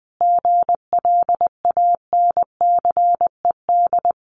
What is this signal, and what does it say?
Morse code, heard as a keyed tone, about 20 wpm, 700 Hz. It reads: GLANCED
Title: A Sound Of Thunder 20wpm